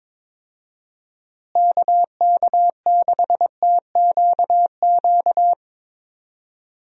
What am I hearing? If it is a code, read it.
Morse code, heard as a keyed tone, about 22 wpm, 700 Hz. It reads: KK6TQQ
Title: USA Call Signs - 6 Characters - Repeated 3x - 22wpm